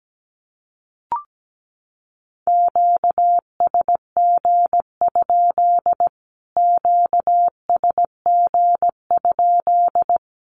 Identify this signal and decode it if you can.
Morse code, heard as a keyed tone, about 17 wpm, 700 Hz. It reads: QSG? QSG?
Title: Q-Codes Repeated 3x 17wpm